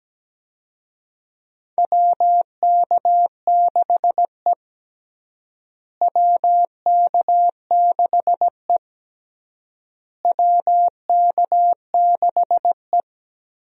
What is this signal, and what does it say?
Morse code, heard as a keyed tone, about 17 wpm, 700 Hz. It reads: WK6E WK6E WK6E
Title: USA Call Signs - 4 Characters - Repeated 3x - 17wpm